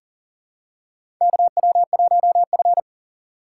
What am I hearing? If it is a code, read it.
Morse code, heard as a keyed tone, about 37 wpm, 700 Hz. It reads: KW1F
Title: USA Call Signs - 4 Characters - 3x Character Spacing - 40wpm